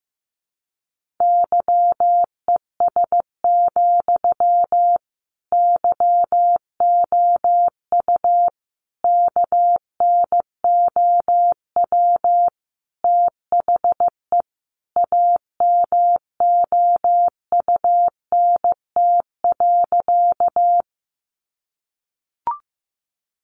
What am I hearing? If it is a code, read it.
Morse code, heard as a keyed tone, about 15 wpm, 700 Hz. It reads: YES, YOU KNOW THE AMOUNT.
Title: Sentences from Top 700 Words 15wpm